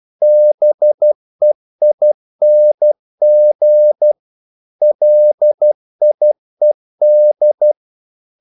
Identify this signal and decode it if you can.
Morse code, heard as a keyed tone, about 12 wpm, 600 Hz. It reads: BEING LIED
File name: CW / Practice / If12